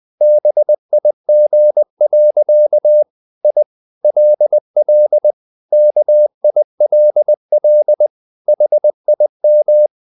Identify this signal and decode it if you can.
Morse code, heard as a keyed tone, about 20 wpm, 600 Hz. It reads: BIG. I LL KILL HIM
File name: the old man and the sea / the old man and the sea part 1 20wpm